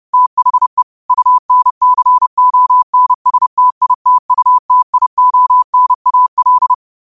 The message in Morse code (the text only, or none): THE UNCONSTITUTIONAL